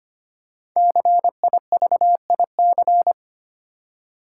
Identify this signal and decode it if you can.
Morse code, heard as a keyed tone, about 25 wpm, 700 Hz. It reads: CIVIC
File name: 5 Letter Words - First 3 Letters Given - 25wpm